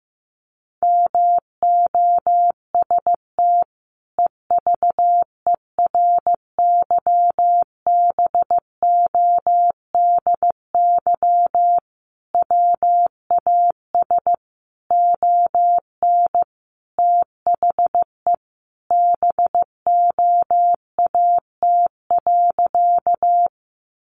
Morse code, heard as a keyed tone, about 15 wpm, 700 Hz. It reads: MOST EVERYBODY WAS ON THE BOAT.
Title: Sentences from Top 1400 Words 15wpm